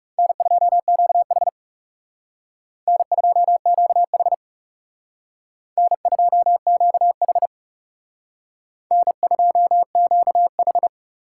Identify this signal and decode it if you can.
Morse code, heard as a keyed tone, about 38 wpm, 700 Hz. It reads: N2QH N2QH N2QH N2QH